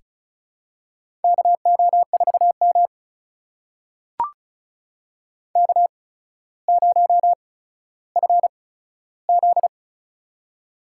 Morse code, heard as a keyed tone, about 36 wpm, spaced as for 15 wpm, 700 Hz. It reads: KO4M K0FZ
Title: USA Call Signs - 4 Characters - 8x Character Spacing - 35wpm